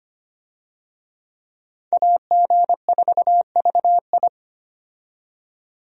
Morse code, heard as a keyed tone, about 25 wpm, 700 Hz. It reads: AG4VI
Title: USA Call Signs - 5 Characters - 25wpm